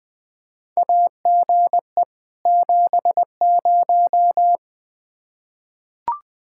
Morse code, heard as a keyed tone, about 20 wpm, 700 Hz. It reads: AGE 70
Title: Practice QSO - Exchange 3 - More Info - 20wpm